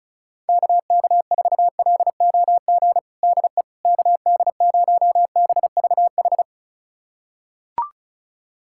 Morse code, heard as a keyed tone, about 35 wpm, 700 Hz. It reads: KK4LOG DE KD0BVH
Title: USA Callsign DE Callsign - 6 Characters - 35wpm